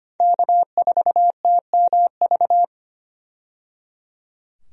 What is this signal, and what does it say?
Morse code, heard as a keyed tone, about 25 wpm, 700 Hz. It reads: K4TMV
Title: USA Call Signs - 5 Characters - 25wpm